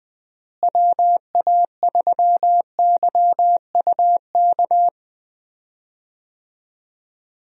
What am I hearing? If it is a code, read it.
Morse code, heard as a keyed tone, about 20 wpm, 700 Hz. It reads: WA3YUK
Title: USA Call Signs - 6 Characters - Repeated 2x - 20wpm